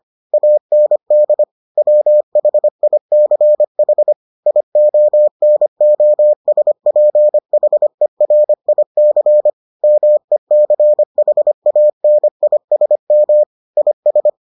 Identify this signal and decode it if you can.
Morse code, heard as a keyed tone, about 25 wpm, 600 Hz. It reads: AND WHICH IONOSPHERIC MECHANISM IS